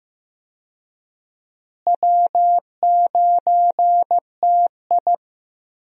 Morse code, heard as a keyed tone, about 15 wpm, 700 Hz. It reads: W9TI
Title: USA Call Signs - 4 Characters - Repeated 3x - 15wpm